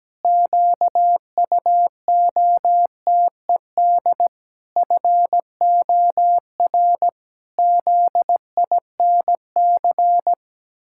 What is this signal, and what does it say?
Morse code, heard as a keyed tone, about 17 wpm, 700 Hz. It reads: QUOTED FOR ZINC
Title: Pangram Challenge v2 - 17wpm